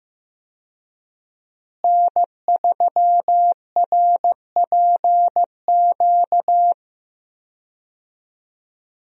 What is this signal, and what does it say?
Morse code, heard as a keyed tone, about 15 wpm, 700 Hz. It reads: N3RPQ
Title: USA Call Signs - 5 Characters - Repeated 3x - 15wpm